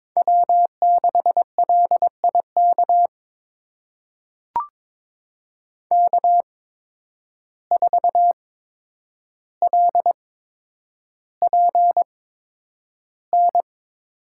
Morse code, heard as a keyed tone, about 22 wpm, spaced as for 9 wpm, 700 Hz. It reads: W6LIK K4LPN
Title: USA Call Signs - 5 Characters - 8x Character Spacing - 22wpm